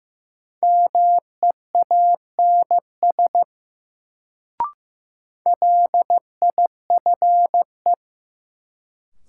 Morse code, heard as a keyed tone, about 15 wpm, 700 Hz. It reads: MEANS LIFE
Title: Sets of 5 Words - New Word Drill - 15wpm